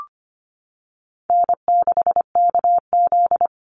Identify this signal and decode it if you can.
Morse code, heard as a keyed tone, about 25 wpm, 700 Hz. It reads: N6KZ
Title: NA SOTA Chaser Callsigns 25wpm